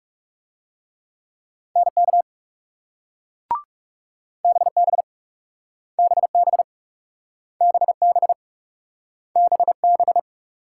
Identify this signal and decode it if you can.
Morse code, heard as a keyed tone, about 38 wpm, 700 Hz. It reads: NK BB BB BB BB